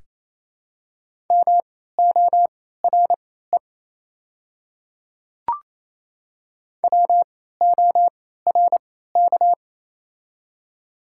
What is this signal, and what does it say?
Morse code, heard as a keyed tone, about 28 wpm, spaced as for 16 wpm, 700 Hz. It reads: MORE WORK
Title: 3x Character Spacing - Top 200 Words - 28wpm